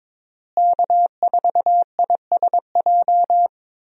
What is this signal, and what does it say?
Morse code, heard as a keyed tone, about 21 wpm, 700 Hz. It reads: K4ISJ
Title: USA Call Signs - 5 Characters - Repeated 2x - 22wpm